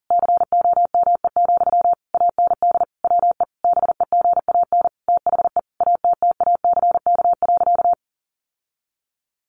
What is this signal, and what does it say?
Morse code, heard as a keyed tone, about 40 wpm, 700 Hz. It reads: COME, AND WE BEGAN THE ATTACK.
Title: Sentences from Top 1500 Words 40wpm